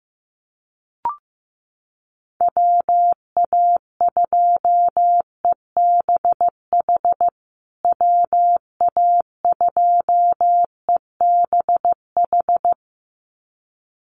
Morse code, heard as a keyed tone, about 15 wpm, 700 Hz. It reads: WA2EBH WA2EBH
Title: USA Call Signs Repeated 2x 15wpm